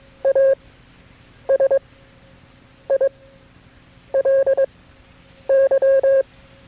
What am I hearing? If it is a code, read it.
Morse code, heard as a keyed tone, about 22 wpm, spaced as for 7 wpm, 540 Hz. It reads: ASILY
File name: FreeFiles / CW-practice / KB6MT-tape3-side1